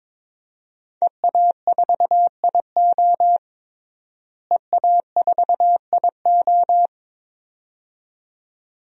Morse code, heard as a keyed tone, about 22 wpm, 700 Hz. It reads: EA4IO EA4IO
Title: DXCC-100 Call Signs - 5 Characters - Repeated 2x - 22wpm